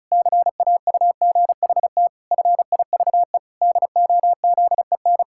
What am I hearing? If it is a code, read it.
Morse code, heard as a keyed tone, about 36 wpm, 700 Hz. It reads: CAUGHT FIVE DOZEN
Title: Pangram Challenge v2 - 35wpm